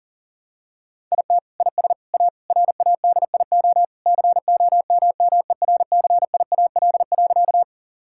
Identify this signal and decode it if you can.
Morse code, heard as a keyed tone, about 37 wpm, 700 Hz. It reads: IT IS A RADIO COMMERCIAL.
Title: Sentences from Top 1000 Words - Review Part 5 - 40wpm